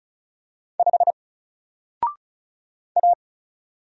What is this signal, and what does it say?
Morse code, heard as a keyed tone, about 35 wpm, 700 Hz. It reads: 5 A
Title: Single Letter-Number Alt 35wpm